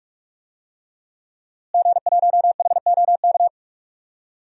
Morse code, heard as a keyed tone, about 46 wpm, 700 Hz. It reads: G1HOK
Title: DXCC-100 Call Signs - 5 Characters - Repeated 3x - 45wpm